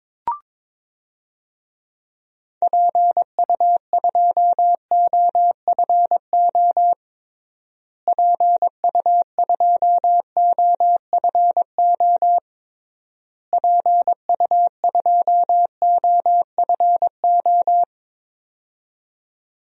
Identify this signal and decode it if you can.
Morse code, heard as a keyed tone, about 22 wpm, 700 Hz. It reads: PU2OFO PU2OFO PU2OFO
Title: DXCC-100 Call Signs - 6 Characters - Repeated 3x - 22wpm